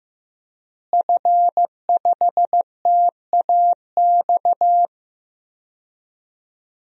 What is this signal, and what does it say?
Morse code, heard as a keyed tone, about 15 wpm, 700 Hz. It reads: F5TAX
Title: DXCC-100 Call Signs - 5 Characters - Repeated 2x - 15wpm